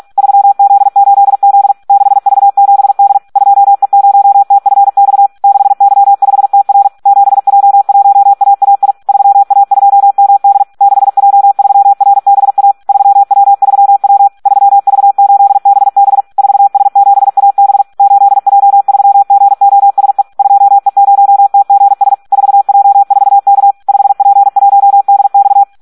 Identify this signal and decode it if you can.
Morse code, heard as a keyed tone, about 46 wpm, 800 Hz. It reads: 3Z8Z 6V7D 2E0TFX 6Y5TR 7J1AAI 3A2MD 6J3RBA 3W3W 3V8BB 4S7AB 8J3GOSE 2E0TGS 4J4K 5P1DX
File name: Audio practica CW / SD 1 46